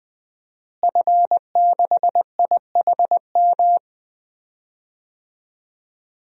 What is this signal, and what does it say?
Morse code, heard as a keyed tone, about 20 wpm, 700 Hz. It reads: F6IHM
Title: DXCC-100 Call Signs - 5 Characters - Repeated 3x - 20wpm